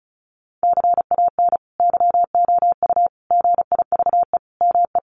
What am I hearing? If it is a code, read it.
Morse code, heard as a keyed tone, about 34 wpm, 700 Hz. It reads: CAN YOU GIVE ME
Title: Sentences from Top 1500 Words 35wpm